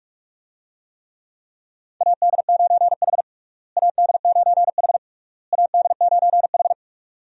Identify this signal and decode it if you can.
Morse code, heard as a keyed tone, about 44 wpm, 700 Hz. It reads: AD9H AD9H AD9H
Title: USA Call Signs - 4 Characters - Repeated 3x - 45wpm